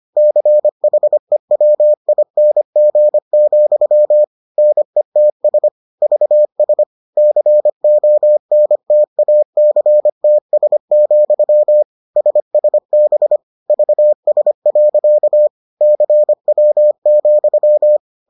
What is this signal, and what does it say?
Morse code, heard as a keyed tone, about 25 wpm, 600 Hz. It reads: CHEWING, NETS VS CONTACTS, SSB VS. CW,